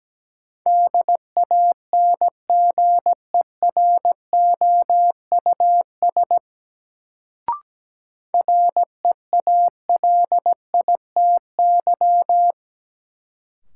Morse code, heard as a keyed tone, about 17 wpm, 700 Hz. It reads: DANGEROUS REALITY
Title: Sentences from Top 1200 Words - New Word Drill Alt - 17wpm